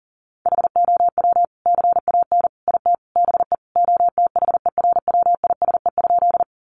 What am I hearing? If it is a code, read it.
Morse code, heard as a keyed tone, about 37 wpm, 700 Hz. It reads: HOW CAN IT BE OTHERWISE?
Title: Sentences from Top 1200 Words 40wpm